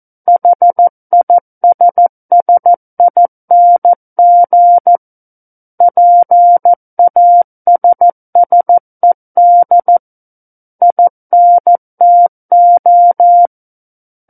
Morse code, heard as a keyed tone, about 14 wpm, 700 Hz. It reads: HISSING PASSED INTO
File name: Book One / 05-Chapter 5 - 14 WPM